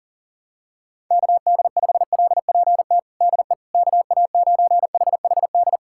Morse code, heard as a keyed tone, about 40 wpm, 700 Hz. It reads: KD5LPT DE KA9HHD
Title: USA Callsign DE Callsign 40wpm